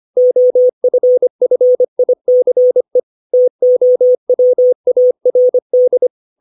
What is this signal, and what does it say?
Morse code, heard as a keyed tone, about 25 wpm, 500 Hz. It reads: OFFICE TOWARD
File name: 632 words / 632 Words 25WPM